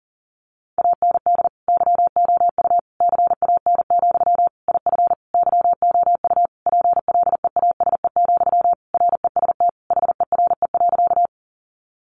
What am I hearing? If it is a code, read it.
Morse code, heard as a keyed tone, about 40 wpm, 700 Hz. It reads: AND YOU CAN, IF YOU PLEASE, REST HERE.
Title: Sentences from Top 1500 Words 40wpm